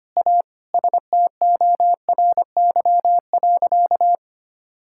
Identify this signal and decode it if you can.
Morse code, heard as a keyed tone, about 25 wpm, 700 Hz. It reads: A STORY.
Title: Sentences from Top 1000 Words - Review Part 6 - 25wpm